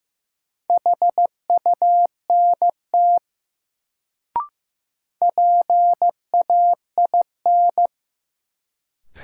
Morse code, heard as a keyed tone, about 15 wpm, 700 Hz. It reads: HUNT PAIN
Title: Sentences from Top 1600 Words - New Word Drill Alt - 15wpm